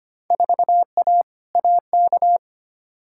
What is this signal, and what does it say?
Morse code, heard as a keyed tone, about 25 wpm, 700 Hz. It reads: 4A AK
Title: Field Day Exchanges Encore - 25wpm